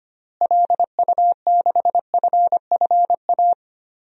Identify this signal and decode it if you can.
Morse code, heard as a keyed tone, about 25 wpm, 700 Hz. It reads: LU6FFA
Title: DXCC-100 Call Signs - 6 Characters - Repeated 3x - 25wpm